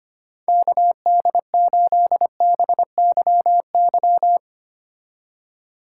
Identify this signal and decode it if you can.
Morse code, heard as a keyed tone, about 31 wpm, 700 Hz. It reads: KD8BYY
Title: Speed Racing - USA 2x3 Call Signs 25wpm